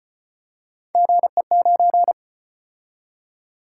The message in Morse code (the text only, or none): GE9